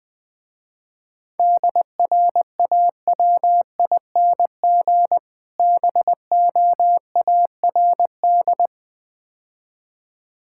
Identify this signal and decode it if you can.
Morse code, heard as a keyed tone, about 21 wpm, 700 Hz. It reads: DRAWING BOARD
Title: Difficult Words - Speed-Racing with Hints - 20wpm